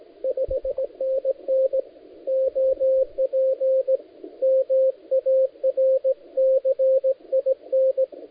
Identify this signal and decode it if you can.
Morse code, heard as a keyed tone, about 18 wpm, 530 Hz, with radio noise. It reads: 5NN OP MARCIN
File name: dlazaawansowanych / nagraniazpasma / titawaNETcontrol